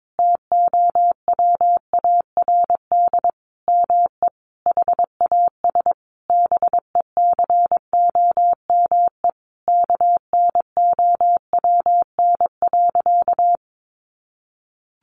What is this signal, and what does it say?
Morse code, heard as a keyed tone, about 22 wpm, 700 Hz. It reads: TOWARD ME HAS BECOME KNOWN.
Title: Sentences from Top 1300 Words 22wpm